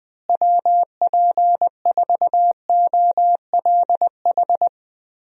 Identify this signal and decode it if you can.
Morse code, heard as a keyed tone, about 20 wpm, 700 Hz. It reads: WP4OLH
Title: USA Call Signs - 6 Characters - Repeated 3x - 20wpm